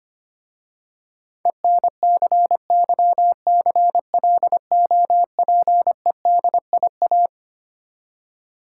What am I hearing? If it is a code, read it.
Morse code, heard as a keyed tone, about 25 wpm, 700 Hz. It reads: ENCYCLOPEDIA
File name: practice / 25wpm / Difficult Words with Hints - 25wpm